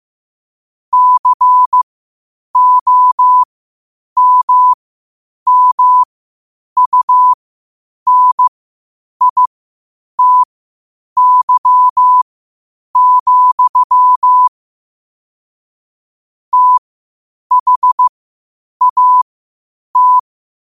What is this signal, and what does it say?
Morse code, heard as a keyed tone, about 15 wpm, spaced as for 8 wpm, 1000 Hz. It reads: COMMUNITY, THAT
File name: qotdfiles / 8wpm